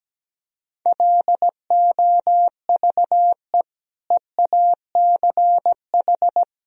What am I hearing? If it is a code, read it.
Morse code, heard as a keyed tone, about 17 wpm, 700 Hz. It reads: LOVE EACH
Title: Sentences from Top 1000 Words - Review Part 1 - 17wpm